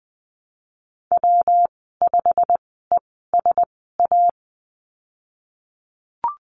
Morse code, heard as a keyed tone, about 20 wpm, spaced as for 15 wpm, 700 Hz. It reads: W5ESA
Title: 207 Lesson 33g - Letter j - Call Signs 20wpm